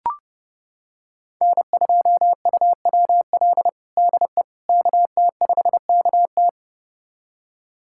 Morse code, heard as a keyed tone, about 32 wpm, 700 Hz. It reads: N2UWL DE KT5KT
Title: USA Callsign DE Callsign - 5 Characters - 30wpm